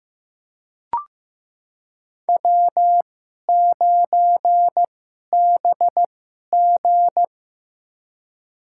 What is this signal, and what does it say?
Morse code, heard as a keyed tone, about 15 wpm, spaced as for 11 wpm, 700 Hz. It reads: W9BG
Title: USA Call Signs - 4 Characters - 2x Character Spacing - 15wpm